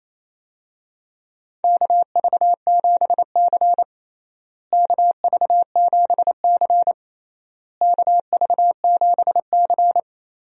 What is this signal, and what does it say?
Morse code, heard as a keyed tone, about 28 wpm, 700 Hz. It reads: KV7C KV7C KV7C
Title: USA Call Signs - 4 Characters - Repeated 3x - 28wpm